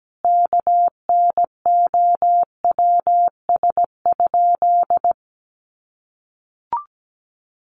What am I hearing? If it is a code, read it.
Morse code, heard as a keyed tone, about 17 wpm, 700 Hz. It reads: KNOWS?
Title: A Sound Of Thunder 17wpm